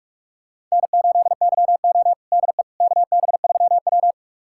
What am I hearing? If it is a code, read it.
Morse code, heard as a keyed tone, about 43 wpm, 700 Hz. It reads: N8YO DE KB3W
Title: USA Callsign DE Callsign - 4 Characters - 45wpm